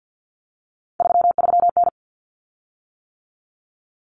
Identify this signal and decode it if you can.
Morse code, heard as a keyed tone, about 48 wpm, 700 Hz. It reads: 33N
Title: Common Cut Numbers - 3-Digit 50wpm